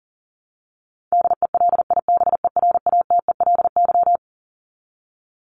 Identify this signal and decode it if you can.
Morse code, heard as a keyed tone, about 40 wpm, 700 Hz. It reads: DELIBERATELY
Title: Top 1751-2000 Words 40wpm